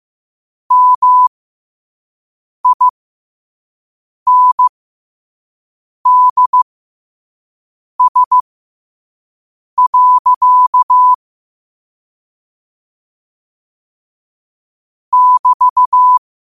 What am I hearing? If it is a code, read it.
Morse code, heard as a keyed tone, about 15 wpm, spaced as for 5 wpm, 1000 Hz. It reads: MINDS. <BT>